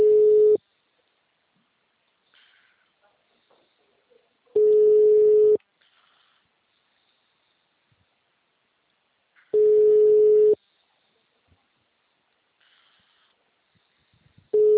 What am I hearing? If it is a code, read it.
Ringback tone (Europe, 425 Hz).
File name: out- -2007- - - .1132